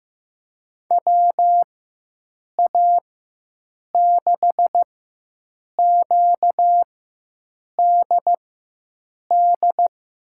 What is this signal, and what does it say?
Morse code, heard as a keyed tone, about 15 wpm, spaced as for 7 wpm, 700 Hz. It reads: WA6QDD
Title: USA Call Signs - 6 Characters - 4x Character Spacing - 15wpm